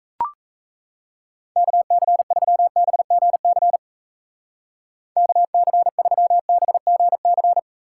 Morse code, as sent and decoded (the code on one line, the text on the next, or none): -.- -.-. ...-- -... --. -.-. / -.- -.-. ...-- -... --. -.-.
KC3BGC KC3BGC